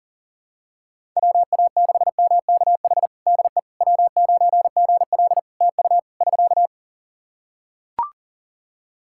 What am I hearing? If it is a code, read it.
Morse code, heard as a keyed tone, about 40 wpm, 700 Hz. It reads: WA6MKH DE W9GL TU <SK>
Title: Practice QSO - Exchange 4 - Ending - 40wpm